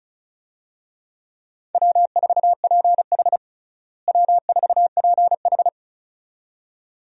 Morse code, heard as a keyed tone, about 36 wpm, 700 Hz. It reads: W4PH W4PH
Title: USA Call Signs - 4 Characters - Repeated 2x - 35wpm